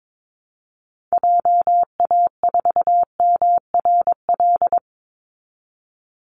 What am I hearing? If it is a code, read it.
Morse code, heard as a keyed tone, about 22 wpm, 700 Hz. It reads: JA4MRL
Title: Call Signs Repeated 3x - 22wpm